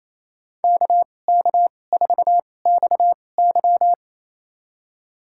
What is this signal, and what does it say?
Morse code, heard as a keyed tone, about 28 wpm, spaced as for 20 wpm, 700 Hz. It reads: KK4XY
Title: USA Call Signs - 5 Characters - 2x Character Spacing - 28wpm